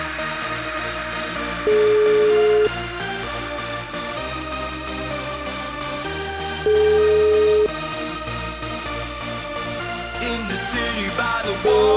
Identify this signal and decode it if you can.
Ringback tone (Europe, 425 Hz), with noise.